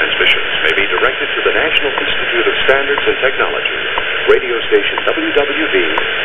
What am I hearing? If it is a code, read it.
Radio time signal, with speech.